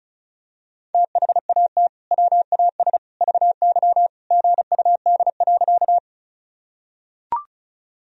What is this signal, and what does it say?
Morse code, heard as a keyed tone, about 36 wpm, 700 Hz. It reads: THAT WAS VY GUD.
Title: Sentences from Top 200 Words and Abbreviations 35wpm